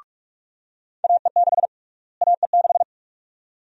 Morse code, heard as a keyed tone, about 46 wpm, 700 Hz. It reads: AE6 AE6
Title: USA Call Sign Prefixes - Repeated 2x - 45wpm